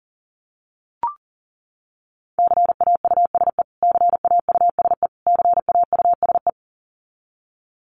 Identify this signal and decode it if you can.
Morse code, heard as a keyed tone, about 40 wpm, 700 Hz. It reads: CAUSE CAUSE CAUSE
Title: Top 500 Words Repeated 40wpm